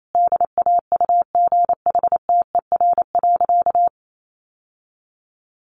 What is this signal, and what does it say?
Morse code, heard as a keyed tone, about 28 wpm, 700 Hz. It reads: DAUGHTER.
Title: Sentences from Top 1500 Words 28wpm